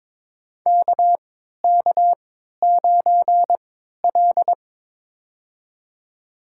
Morse code, heard as a keyed tone, about 22 wpm, spaced as for 13 wpm, 690 Hz. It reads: KK9L
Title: USA Call Signs - 4 Characters - 3x Character Spacing - 22wpm